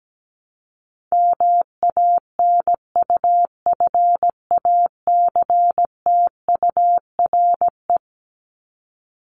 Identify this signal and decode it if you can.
Morse code, heard as a keyed tone, about 17 wpm, 700 Hz. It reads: MANUFACTURE
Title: ING Suffix - Top 100 Words with Base Words 17wpm